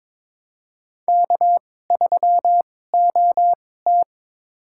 Morse code, heard as a keyed tone, about 22 wpm, spaced as for 16 wpm, 700 Hz. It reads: K3OT
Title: USA Call Signs - 2x Character Spacing - 22wpm